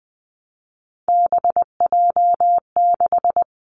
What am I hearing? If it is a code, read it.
Morse code, heard as a keyed tone, about 20 wpm, 700 Hz. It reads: BJ6
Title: Three Challenging Letter-Number Combinations 20wpm